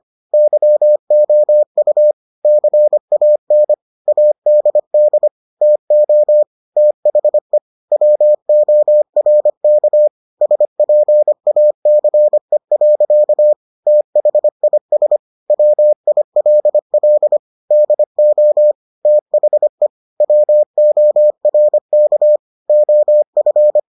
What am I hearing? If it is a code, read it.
Morse code, heard as a keyed tone, about 25 wpm, 600 Hz. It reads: YOU CAN ADD TO THE WORK SPACE. THIS WILL DO THE WORK OF